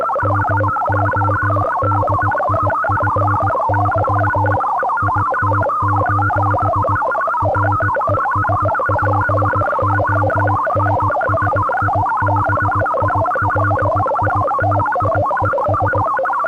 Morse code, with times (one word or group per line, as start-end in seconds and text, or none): 0.2-4.6 MODIFY
5.0-7.0 IT:
7.4-16.0 RESPONSIBILITIES